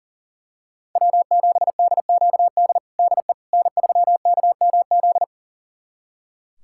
Morse code, heard as a keyed tone, about 40 wpm, 700 Hz. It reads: W7DQD DE N3KMZ
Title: USA Callsign DE Callsign - 5 Characters - 40wpm